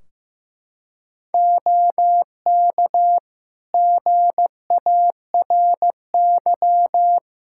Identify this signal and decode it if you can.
Morse code, heard as a keyed tone, about 15 wpm, 700 Hz. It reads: OK GARY
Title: Practice QSO - Exchange 4 - Ending - 15wpm